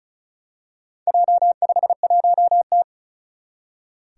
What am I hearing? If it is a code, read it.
Morse code, heard as a keyed tone, about 36 wpm, 700 Hz. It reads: J51T
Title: Four Challenging Letter-Number Combinations 35wpm